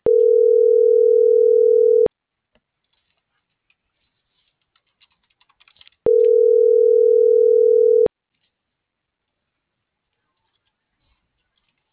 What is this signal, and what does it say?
Ringback tone (North America), 440 + 480 Hz.